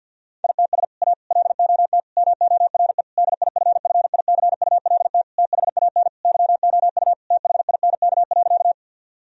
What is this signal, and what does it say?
Morse code, heard as a keyed tone, about 50 wpm, 700 Hz. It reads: ITS A LOT MORE DIFFICULT THAN YOU THINK.
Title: Sentences from Top 1000 Words - Review Part 3 - 50wpm